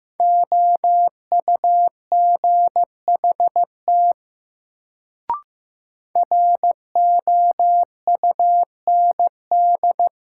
Morse code, heard as a keyed tone, about 15 wpm, 700 Hz. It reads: OUGHT ROUND
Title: Sets of 5 Words Encore - New Word Drill - 15wpm